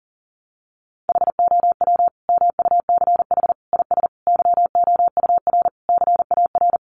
Morse code, heard as a keyed tone, about 40 wpm, 700 Hz. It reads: HOW MUCH IS YOUR CAR
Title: Sentences from Top 1100 Words 40wpm